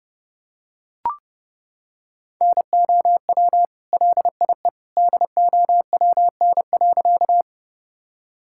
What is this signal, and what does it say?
Morse code, heard as a keyed tone, about 30 wpm, 700 Hz. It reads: NOW LIE DOWN.
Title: Sentences from Top 1600 Words 30wpm